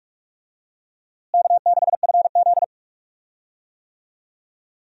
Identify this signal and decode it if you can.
Morse code, heard as a keyed tone, about 46 wpm, 700 Hz. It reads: K6FZ
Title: USA Call Signs - 4 Characters - Repeated 2x - 45wpm